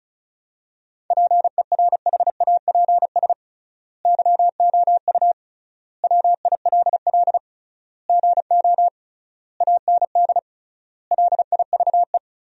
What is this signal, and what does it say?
Morse code, heard as a keyed tone, about 34 wpm, 700 Hz. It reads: PERHAPS YOU WILL GO AND LIVE
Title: 3x Word-Spacing - Sentences from Top 500 Words - 35wpm